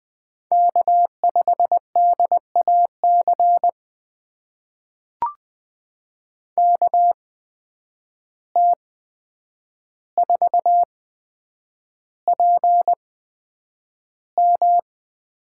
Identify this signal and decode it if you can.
Morse code, heard as a keyed tone, about 20 wpm, spaced as for 9 wpm, 700 Hz. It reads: K5DAC KT4PM
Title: USA Call Signs - 5 Characters - 8x Character Spacing - 20wpm